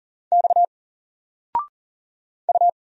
Morse code, heard as a keyed tone, about 40 wpm, 700 Hz. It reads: X U
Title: Single Letter-Number Alt 40wpm